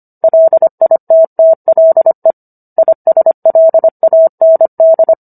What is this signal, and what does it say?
Morse code, heard as a keyed tone, about 25 wpm, 650 Hz. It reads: LITTLE ISLAND